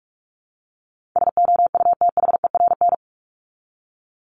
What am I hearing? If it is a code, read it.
Morse code, heard as a keyed tone, about 44 wpm, 700 Hz. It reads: SOUTHERN
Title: Top 1751-2000 Words 45wpm